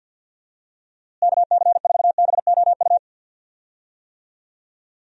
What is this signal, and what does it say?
Morse code, heard as a keyed tone, about 50 wpm, 700 Hz. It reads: KC4BOU
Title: USA Call Signs - 6 Characters - Repeated 2x - 50wpm